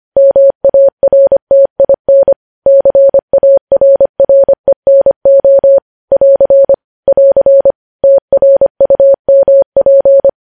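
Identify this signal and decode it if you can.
Morse code, heard as a keyed tone, about 25 wpm, 550 Hz. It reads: MARTIN CARRENO <AR> <AR> TRUMP
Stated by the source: Morse code news headlines at 25, opening with vvv.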